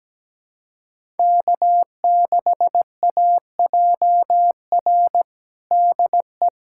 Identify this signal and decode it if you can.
Morse code, heard as a keyed tone, about 17 wpm, 700 Hz. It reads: K6AJR DE
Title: USA Callsign DE Callsign - 5 Characters - 17wpm